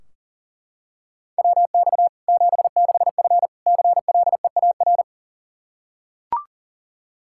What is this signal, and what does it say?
Morse code, heard as a keyed tone, about 37 wpm, 700 Hz. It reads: WX 76F CLEAR
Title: K1USN SST QSO Practice 40wpm